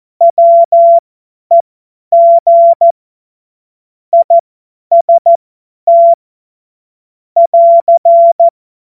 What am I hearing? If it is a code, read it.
Morse code, heard as a keyed tone, about 14 wpm, spaced as for 10 wpm, 680 Hz. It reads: WEG IST <AR>
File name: Morsekurs / Witze / Witz 24.Oktober 2023 50 F70